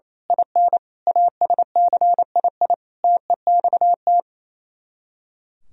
Morse code, heard as a keyed tone, about 27 wpm, 700 Hz. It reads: IN ASCII TEXT
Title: Speed-Racing - Sets of 5 Words 28wpm